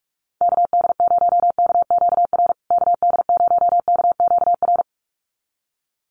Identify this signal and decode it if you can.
Morse code, heard as a keyed tone, about 44 wpm, 700 Hz. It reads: KD0KQR KD0KQR
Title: USA Call Signs Repeated 2x 45wpm